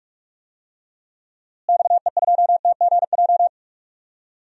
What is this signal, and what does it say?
Morse code, heard as a keyed tone, about 46 wpm, 700 Hz. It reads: XE2TGJ
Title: DXCC-100 Call Signs - 6 Characters - Repeated 2x - 45wpm